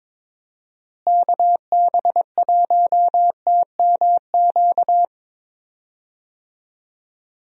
Morse code, heard as a keyed tone, about 22 wpm, 700 Hz. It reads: KB1TMQ
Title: USA Call Signs - 6 Characters - Repeated 2x - 22wpm